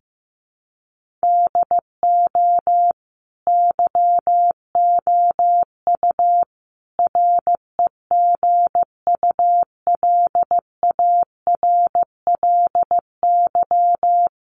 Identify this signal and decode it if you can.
Morse code, heard as a keyed tone, about 15 wpm, 700 Hz. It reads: DO YOU REGULARLY
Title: Sentences from Top 1400 Words 15wpm